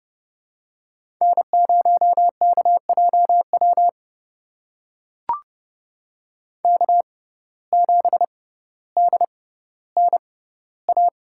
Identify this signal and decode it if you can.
Morse code, heard as a keyed tone, about 30 wpm, spaced as for 15 wpm, 700 Hz. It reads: N0KJW K7DNA
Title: USA Call Signs - 5 Characters - 6x Character Spacing - 30wpm